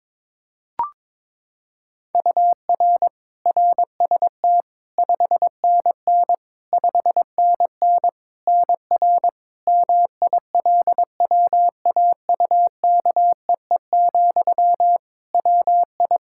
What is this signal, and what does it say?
Morse code, heard as a keyed tone, about 21 wpm, 700 Hz. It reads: UR RST 5NN 5NN NR MILWAUKEE, WI
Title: Practice QSO - Exchange 1 - Beginning - 22wpm